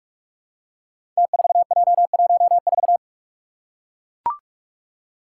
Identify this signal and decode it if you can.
Morse code, heard as a keyed tone, about 44 wpm, 700 Hz. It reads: T4J14
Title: Five Challenging Letter-Number Combinations 45wpm